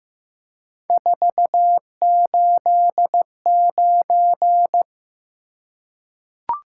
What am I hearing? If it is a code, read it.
Morse code, heard as a keyed tone, about 15 wpm, 700 Hz. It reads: 489
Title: Top 180 QSO Elements 15wpm